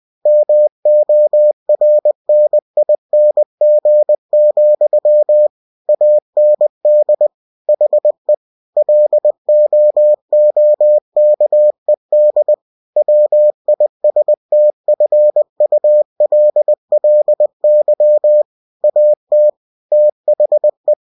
Morse code, heard as a keyed tone, about 20 wpm, 600 Hz. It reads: MORNING, AND HE LOOKED WISTFULLY AT THE
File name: cw-books / winnie-the-pooh / winnie-20wpm-page-0030